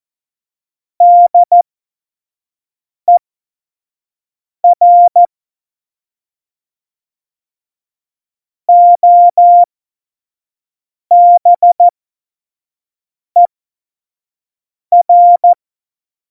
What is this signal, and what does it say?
Morse code, heard as a keyed tone, about 14 wpm, spaced as for 5 wpm, 700 Hz. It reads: DER OBER
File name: Morsekurs / Witze / Witz 15. Oktober 2019, 25 F70